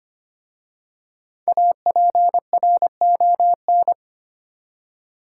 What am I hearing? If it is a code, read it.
Morse code, heard as a keyed tone, about 25 wpm, 700 Hz. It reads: APRON
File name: practice / 25wpm / Easy Words with Hints - 25wpm